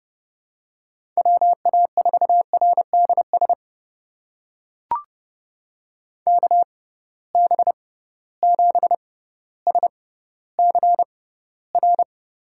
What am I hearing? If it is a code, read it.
Morse code, heard as a keyed tone, about 30 wpm, spaced as for 15 wpm, 700 Hz. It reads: WA4RDS KB7SCR
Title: USA Call Signs - 6 Characters - 6x Character Spacing - 30wpm